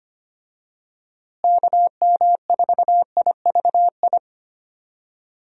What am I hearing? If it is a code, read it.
Morse code, heard as a keyed tone, about 25 wpm, 700 Hz. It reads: KM4IVI
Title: USA Call Signs - 6 Characters - Repeated 2x - 25wpm